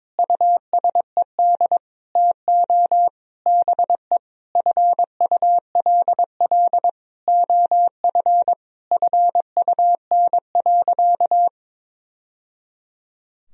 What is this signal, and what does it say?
Morse code, heard as a keyed tone, about 22 wpm, 700 Hz. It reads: USED TO BE FULL OF FUN.
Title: Sentences from Top 1000 Words - Review Part 4 - 22wpm